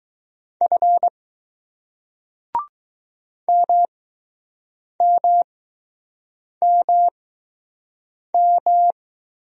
Morse code, heard as a keyed tone, about 20 wpm, spaced as for 6 wpm, 700 Hz. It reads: F MMMM